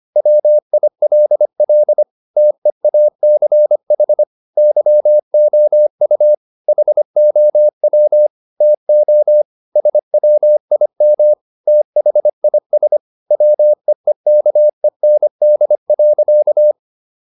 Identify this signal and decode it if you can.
Morse code, heard as a keyed tone, about 25 wpm, 600 Hz. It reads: WILL TEACH YOU HOW TO SWIM THIS WEEKEND.